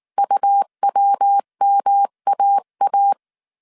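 Morse code, heard as a keyed tone, about 19 wpm, 800 Hz. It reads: UWMAA
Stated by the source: Morse code dictation at 18 wpm from the IW3SGT QRS Morse code training course, five-character groups of 300 characters.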